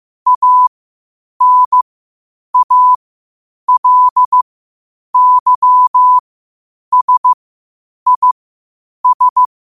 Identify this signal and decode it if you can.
Morse code, heard as a keyed tone, about 15 wpm, spaced as for 8 wpm, 1000 Hz. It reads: ANALYSIS